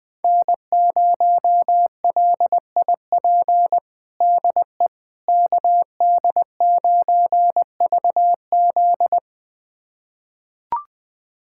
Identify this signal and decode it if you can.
Morse code, heard as a keyed tone, about 20 wpm, 700 Hz. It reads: N0LIP DE KD9VZ
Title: USA Callsign DE Callsign - 5 Characters - 20wpm